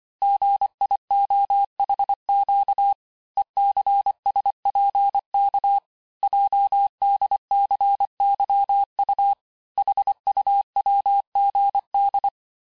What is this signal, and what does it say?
Morse code, heard as a keyed tone, about 25 wpm, 800 Hz. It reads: GIOHQ ECSPK JDCYU HUWGD
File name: texte-chiffre-vitesse-23mpm